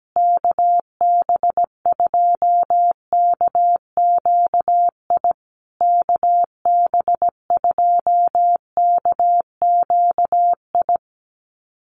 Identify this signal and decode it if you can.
Morse code, heard as a keyed tone, about 17 wpm, 700 Hz. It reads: KB2KQI KB2KQI
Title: USA Call Signs Repeated 3x 17wpm